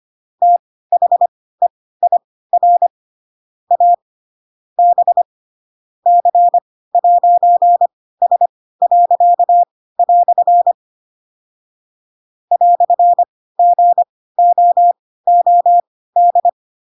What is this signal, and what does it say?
Morse code, heard as a keyed tone, about 25 wpm, spaced as for 16 wpm, 700 Hz. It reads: THEIR A B C'S." "GOOD
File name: Pinocchio / Pinocchio ch2 10 25 ik1hge